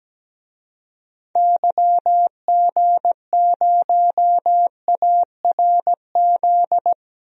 Morse code, heard as a keyed tone, about 17 wpm, 700 Hz. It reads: YG0ARZ
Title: DXCC-100 Call Signs - 6 Characters - 17wpm